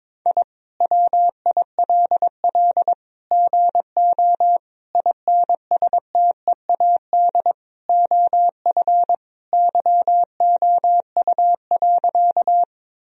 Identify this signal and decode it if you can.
Morse code, heard as a keyed tone, about 21 wpm, 700 Hz. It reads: I WILL GO INSTEAD OF YOU.
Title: Sentences from Top 1000 Words - Review Part 4 - 22wpm